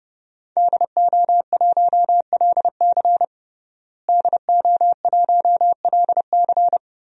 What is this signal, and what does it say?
Morse code, heard as a keyed tone, about 30 wpm, 700 Hz. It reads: DO1LC DO1LC
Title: DXCC-100 Call Signs - 5 Characters - Repeated 2x - 30wpm